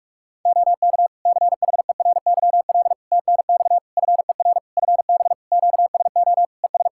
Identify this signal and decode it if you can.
Morse code, heard as a keyed tone, about 46 wpm, 700 Hz. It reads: OK CHERYL TNX FER FB QSO ES
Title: Practice QSO - Exchange 4 - Ending - 45wpm